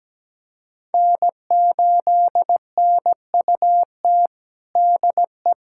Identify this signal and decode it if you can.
Morse code, heard as a keyed tone, about 17 wpm, 700 Hz. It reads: N8NUT DE
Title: USA Callsign DE Callsign - 5 Characters - 17wpm